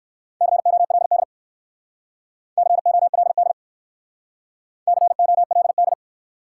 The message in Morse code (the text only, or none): COLD COLD COLD